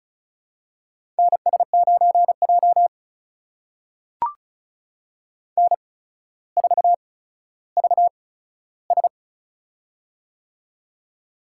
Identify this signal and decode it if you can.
Morse code, heard as a keyed tone, about 35 wpm, spaced as for 15 wpm, 700 Hz. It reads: NS9J N4VS
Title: USA Call Signs - 4 Characters - 8x Character Spacing - 35wpm